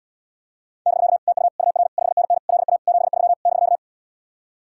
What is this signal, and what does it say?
Morse code, heard as a keyed tone, about 79 wpm, 700 Hz. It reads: <BK> TU UR 5NN LA 73 <BK>